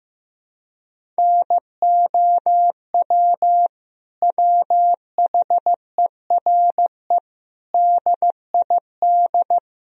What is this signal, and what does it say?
Morse code, heard as a keyed tone, about 15 wpm, 700 Hz. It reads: NOW WHERE DID
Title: Sentences from Top 200 Words and Abbreviations 15wpm